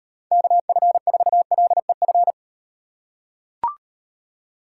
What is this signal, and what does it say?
Morse code, heard as a keyed tone, about 38 wpm, 700 Hz. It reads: KF4LEF